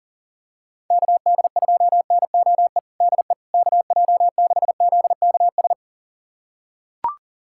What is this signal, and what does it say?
Morse code, heard as a keyed tone, about 37 wpm, 700 Hz. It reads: KD2NOE DE KJ6ZKS
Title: USA Callsign DE Callsign - 6 Characters - 40wpm